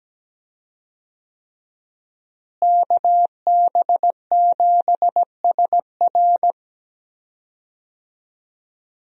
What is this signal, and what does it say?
Morse code, heard as a keyed tone, about 17 wpm, 700 Hz. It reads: KB7SR
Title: USA Call Signs - 5 Characters - Repeated 3x - 17wpm